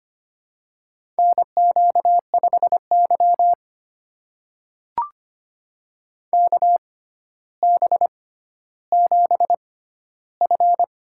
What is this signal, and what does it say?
Morse code, heard as a keyed tone, about 25 wpm, spaced as for 13 wpm, 700 Hz. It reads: NQ5Y KB7F
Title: USA Call Signs - 4 Characters - 6x Character Spacing - 25wpm